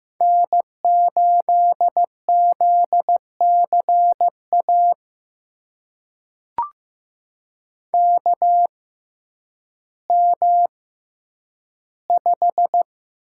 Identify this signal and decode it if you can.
Morse code, heard as a keyed tone, about 15 wpm, spaced as for 9 wpm, 700 Hz. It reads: N8ZCA KM5
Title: USA Call Signs - 5 Characters - 6x Character Spacing - 15wpm